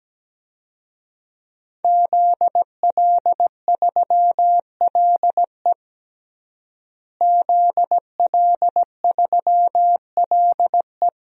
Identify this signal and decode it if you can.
Morse code, heard as a keyed tone, about 17 wpm, 700 Hz. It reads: ZL3LE ZL3LE
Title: DXCC-100 Call Signs - 5 Characters - Repeated 2x - 17wpm